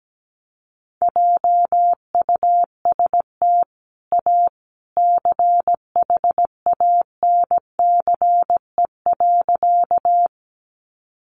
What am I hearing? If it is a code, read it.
Morse code, heard as a keyed tone, about 17 wpm, 700 Hz. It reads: JUST A CHANCE.
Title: Sentences from Top 800 Words 17wpm